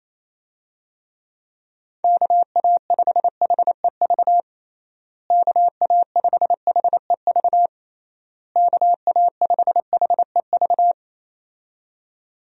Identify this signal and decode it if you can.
Morse code, heard as a keyed tone, about 28 wpm, 700 Hz. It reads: KA5HEV KA5HEV KA5HEV
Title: USA Call Signs - 6 Characters - Repeated 3x - 28wpm